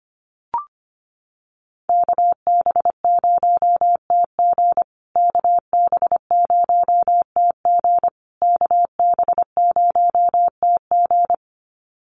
Morse code, heard as a keyed tone, about 25 wpm, 700 Hz. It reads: KB0TG KB0TG KB0TG
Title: USA Call Signs Repeated 3x 25wpm